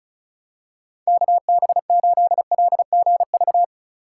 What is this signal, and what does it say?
Morse code, heard as a keyed tone, about 36 wpm, 700 Hz. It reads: KB8LGV
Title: USA Call Signs - 6 Characters - 4x Character Spacing - 35wpm